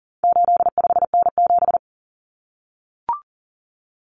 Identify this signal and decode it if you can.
Morse code, heard as a keyed tone, about 40 wpm, 700 Hz. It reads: 85N7
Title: Common Cut Numbers - 4-Digit 40wpm